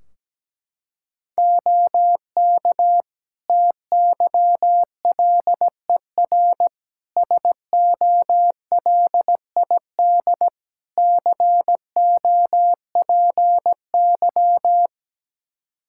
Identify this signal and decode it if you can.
Morse code, heard as a keyed tone, about 17 wpm, 700 Hz. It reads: OK TYLER SOLID COPY
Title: Practice QSO - Exchange 3 - More Info - 17wpm